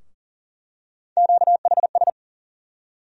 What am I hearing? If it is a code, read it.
Morse code, heard as a keyed tone, about 40 wpm, 700 Hz. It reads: QHS
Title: USA Call Sign Postfixes - 3 Letters 40wpm